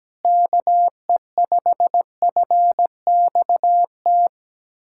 Morse code, heard as a keyed tone, about 17 wpm, 700 Hz. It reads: KE5FXT
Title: USA Callsign DE Callsign - 6 Characters - 17wpm